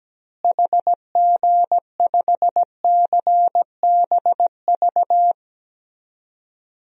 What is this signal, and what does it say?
Morse code, heard as a keyed tone, about 17 wpm, 700 Hz. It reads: HG5CBV
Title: DXCC-100 Call Signs - 6 Characters - Repeated 2x - 17wpm